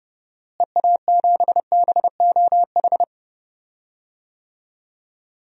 Morse code, heard as a keyed tone, about 30 wpm, 700 Hz. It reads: EA7BOH
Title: DXCC-100 Call Signs - 6 Characters - Repeated 3x - 30wpm